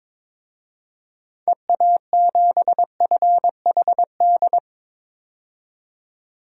Morse code, heard as a keyed tone, about 22 wpm, 700 Hz. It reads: EA7FHD
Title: DXCC-100 Call Signs - 6 Characters - 22wpm